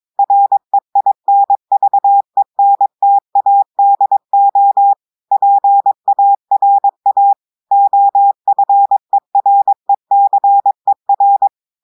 Morse code, heard as a keyed tone, about 22 wpm, 800 Hz. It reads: REINVENTADO PARA OFERECER
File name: audios / portugais / portugais texte en clair 2 vitesse 22 mots minutes